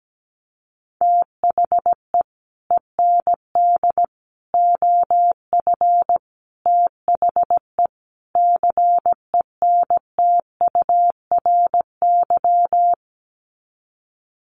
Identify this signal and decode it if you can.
Morse code, heard as a keyed tone, about 17 wpm, 700 Hz. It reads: THE END OF THE CENTURY
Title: Sets of 5 Words Encore2 17wpm